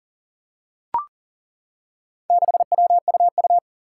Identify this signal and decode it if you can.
Morse code, heard as a keyed tone, about 40 wpm, 700 Hz. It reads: 6WUU
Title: Four Challenging Letter-Number Combinations 40wpm